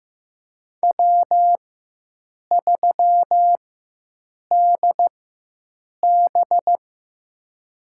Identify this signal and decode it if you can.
Morse code, heard as a keyed tone, about 15 wpm, spaced as for 7 wpm, 700 Hz. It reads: W3DB
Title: USA Call Signs - 4 Characters - 4x Character Spacing - 15wpm